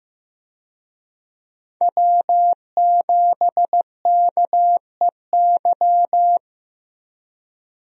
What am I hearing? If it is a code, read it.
Morse code, heard as a keyed tone, about 15 wpm, 700 Hz. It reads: W7KEY
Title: USA Call Signs - 5 Characters - 15wpm